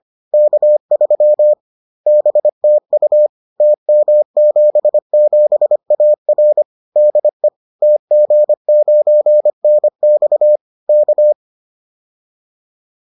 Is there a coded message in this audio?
Yes: Morse code from 0.3 to 11.3 s, heard as a keyed tone, about 25 wpm, 600 Hz. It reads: K3 BTU TM77AR DE TG9NX K